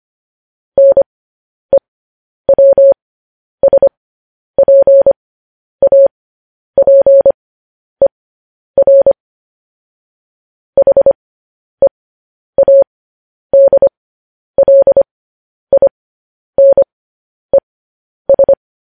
Morse code, heard as a keyed tone, about 25 wpm, spaced as for 10 wpm, 550 Hz. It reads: NEWSPAPER HEADLINES
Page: news headlines in morse code at 10 wpm